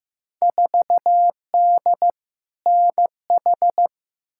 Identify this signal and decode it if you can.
Morse code, heard as a keyed tone, about 15 wpm, 700 Hz. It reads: 4D NH
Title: Field Day Exchanges Encore - 15wpm